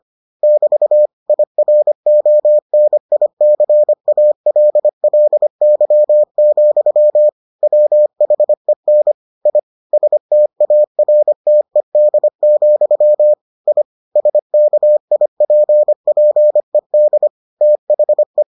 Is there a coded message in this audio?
Yes: Morse code from 0.4 to 18.4 s, heard as a keyed tone, about 25 wpm, 600 Hz. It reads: <BT> IRONICALLY, WHEN I STARTED, I SKIPPED THE